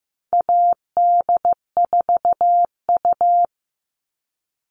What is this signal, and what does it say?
Morse code, heard as a keyed tone, about 15 wpm, 700 Hz. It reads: AD4U
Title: All Cut Numbers - 4-Digit 15wpm